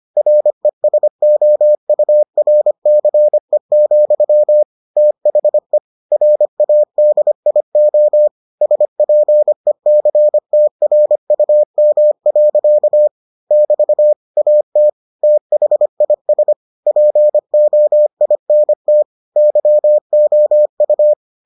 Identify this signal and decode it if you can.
Morse code, heard as a keyed tone, about 25 wpm, 600 Hz. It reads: RESOURCE, THE RADIO SPECTRUM. <BT> AT THIS POINT YOU